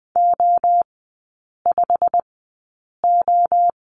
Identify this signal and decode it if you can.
Morse code, heard as a keyed tone, about 20 wpm, spaced as for 8 wpm, 700 Hz. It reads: O5O
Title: 138 Lesson 23e - Letter f - Three Characters 20wpm